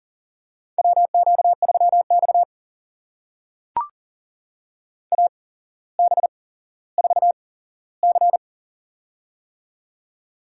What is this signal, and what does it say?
Morse code, heard as a keyed tone, about 40 wpm, spaced as for 17 wpm, 700 Hz. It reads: WQ3X AB4C
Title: USA Call Signs - 8x Character Spacing - 40wpm